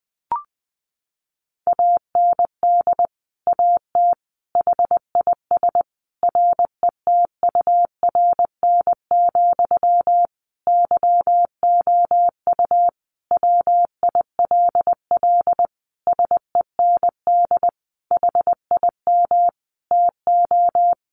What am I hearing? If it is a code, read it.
Morse code, heard as a keyed tone, about 19 wpm, 700 Hz. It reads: AND AT HIS RETURN, YOU WILL SEND HIM TO
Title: Sentences from Top 1500 Words 20wpm